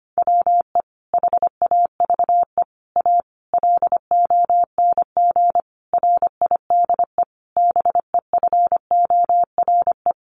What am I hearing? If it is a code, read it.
Morse code, heard as a keyed tone, about 25 wpm, 700 Hz. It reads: WE HAVE A LONG RIDE BEFORE
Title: Sentences from Top 1200 Words 25wpm